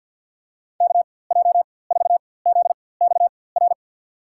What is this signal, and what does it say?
Morse code, heard as a keyed tone, about 52 wpm, spaced as for 23 wpm, 700 Hz. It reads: KJ4ZXR
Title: USA Call Signs - 4x Character Spacing - 50wpm